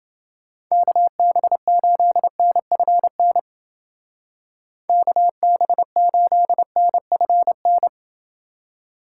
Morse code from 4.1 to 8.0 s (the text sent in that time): KB8NFN